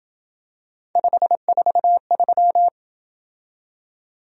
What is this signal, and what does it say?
Morse code, heard as a keyed tone, about 27 wpm, 700 Hz. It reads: 543